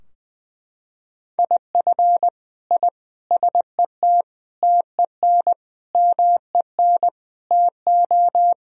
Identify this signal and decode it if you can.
Morse code, heard as a keyed tone, about 20 wpm, 700 Hz. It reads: IF I SET TEN MEN TO
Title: Sentences from Top 1000 Words - Review Part 2 - 20wpm